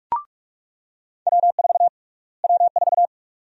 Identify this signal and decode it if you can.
Morse code, heard as a keyed tone, about 44 wpm, 700 Hz. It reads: W4 W4
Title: USA Call Sign Prefixes - Repeated 2x - 45wpm